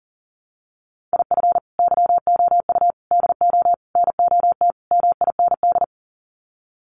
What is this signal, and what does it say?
Morse code, heard as a keyed tone, about 40 wpm, 700 Hz. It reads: IF YOU DO NOT MIND
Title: Sets of 5 Words 40wpm